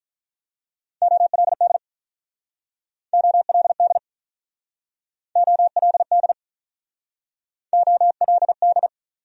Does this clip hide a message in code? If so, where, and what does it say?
Yes: Morse code from 1.0 to 8.9 s, heard as a keyed tone, about 48 wpm, 700 Hz. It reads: OLD OLD OLD OLD